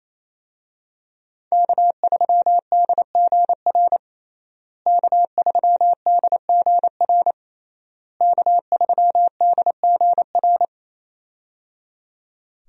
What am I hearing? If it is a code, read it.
Morse code, heard as a keyed tone, about 28 wpm, 700 Hz. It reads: K3DGR K3DGR K3DGR
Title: USA Call Signs - 5 Characters - Repeated 3x - 28wpm